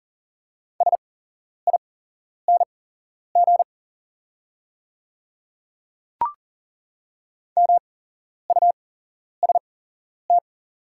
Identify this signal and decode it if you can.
Morse code, heard as a keyed tone, about 40 wpm, spaced as for 11 wpm, 700 Hz. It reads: SING MUST
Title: 8x Character Spacing - Top 500 Words - 40wpm